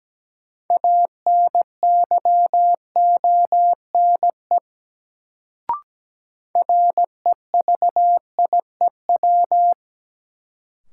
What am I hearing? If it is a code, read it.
Morse code, heard as a keyed tone, about 17 wpm, 700 Hz. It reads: ANYONE REVIEW
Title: Sentences from Top 900 Words - New Word Drill - 17wpm